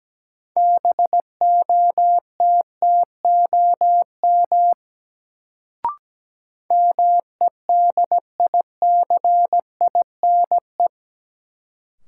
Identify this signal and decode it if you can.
Morse code, heard as a keyed tone, about 17 wpm, 700 Hz. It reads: BOTTOM MEDICINE
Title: Sentences from Top 900 Words - New Word Drill - 17wpm